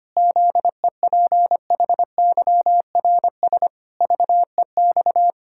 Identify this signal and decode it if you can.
Morse code, heard as a keyed tone, about 25 wpm, 700 Hz. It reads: ZEPHYRS VEX
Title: Pangram Challenge v2 - 25wpm